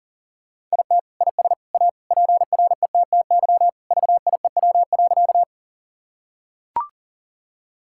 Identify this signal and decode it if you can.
Morse code, heard as a keyed tone, about 40 wpm, 700 Hz. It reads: IT IS A PRETTY VIEW.
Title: Sentences from Top 1000 Words - Review Part 6 - 40wpm